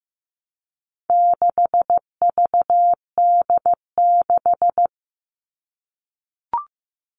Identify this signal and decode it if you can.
Morse code, heard as a keyed tone, about 15 wpm, 700 Hz. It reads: 6VD6
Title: All Cut Numbers - 4-Digit 15wpm